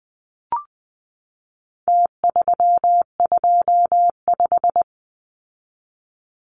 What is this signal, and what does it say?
Morse code, heard as a keyed tone, about 20 wpm, 700 Hz. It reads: T325
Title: Numbers - Contest Style - 4-Digit 20wpm